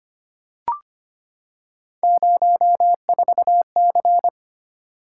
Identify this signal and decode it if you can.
Morse code, heard as a keyed tone, about 25 wpm, 700 Hz. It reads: 04C
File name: Three Letter-Number Combinations - 25wpm